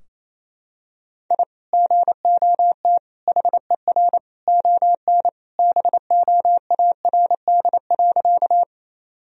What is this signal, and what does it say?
Morse code, heard as a keyed tone, about 28 wpm, 700 Hz. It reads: I GOT HER ON BOARD.
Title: Sentences from Top 1000 Words - Review Part 4 - 28wpm